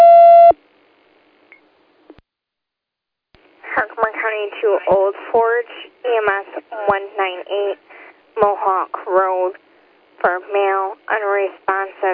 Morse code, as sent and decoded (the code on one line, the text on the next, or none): none